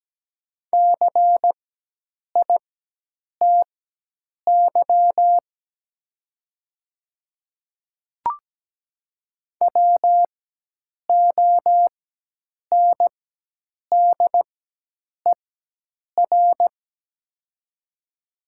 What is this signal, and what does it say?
Morse code, heard as a keyed tone, about 17 wpm, spaced as for 8 wpm, 700 Hz. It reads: CITY WONDER
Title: 4x Character Spacing - Top 500 Words - 17wpm